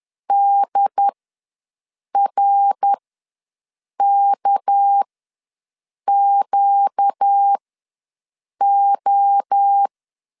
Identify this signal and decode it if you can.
Morse code, heard as a keyed tone, about 10 wpm, spaced as for 6 wpm, 800 Hz. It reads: DRKQO
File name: LEZIONI / 6 PKX0 / 6 PK ws 10wpm 3745